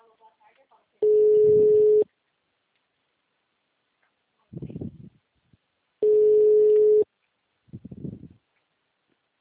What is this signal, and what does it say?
Ringback tone (Europe, 425 Hz), with speech.
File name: out- -2024- - - .30644